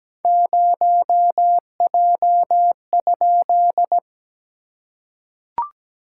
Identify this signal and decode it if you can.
Morse code, heard as a keyed tone, about 17 wpm, 700 Hz. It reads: 0J?
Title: SOTA QSO Practice 17wpm